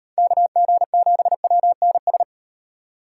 Morse code, heard as a keyed tone, about 36 wpm, 700 Hz. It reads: KG7WNS